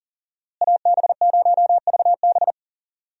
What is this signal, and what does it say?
Morse code, heard as a keyed tone, about 40 wpm, 700 Hz. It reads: AB0VB
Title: USA Call Signs - 5 Characters - Repeated 2x - 40wpm